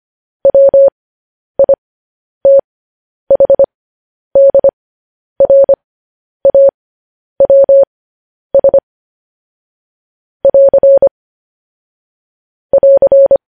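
Morse code, heard as a keyed tone, about 25 wpm, spaced as for 10 wpm, 550 Hz. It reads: WITHDRAWS <AR> <AR>
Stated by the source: Morse code news headlines practice transmission.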